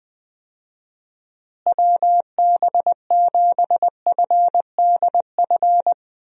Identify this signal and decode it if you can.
Morse code, heard as a keyed tone, about 20 wpm, 700 Hz. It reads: WB7FDF
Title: USA Call Signs - 6 Characters - Repeated 2x - 20wpm